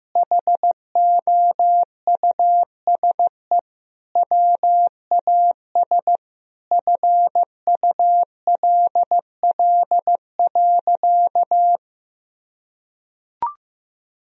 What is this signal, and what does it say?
Morse code, heard as a keyed tone, about 15 wpm, 700 Hz. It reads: HOUSE WAS FULL.
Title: Sentences from Top 1000 Words - Review Part 3 - 15wpm